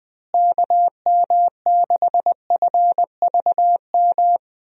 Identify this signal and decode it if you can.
Morse code, heard as a keyed tone, about 20 wpm, 700 Hz. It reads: KM6FVM
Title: USA Call Signs - 6 Characters - Repeated 2x - 20wpm